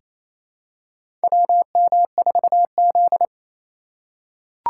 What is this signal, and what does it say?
Morse code, heard as a keyed tone, about 28 wpm, 700 Hz. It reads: WM4Z
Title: USA Call Signs - 4 Characters - 2x Character Spacing - 28wpm